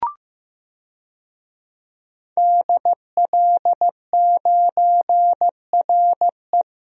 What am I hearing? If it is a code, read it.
Morse code, heard as a keyed tone, about 16 wpm, 700 Hz. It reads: DL9RE
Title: DXCC-100 Call Signs - 5 Characters - Repeated 2x - 15wpm